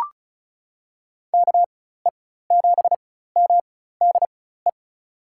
Morse code, heard as a keyed tone, about 36 wpm, spaced as for 16 wpm, 700 Hz. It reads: KE7MDE
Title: USA Call Signs - 6 Characters - 4x Character Spacing - 35wpm